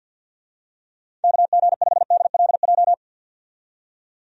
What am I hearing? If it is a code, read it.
Morse code, heard as a keyed tone, about 53 wpm, 700 Hz. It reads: KG5DLJ
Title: USA Call Signs - 6 Characters - 50wpm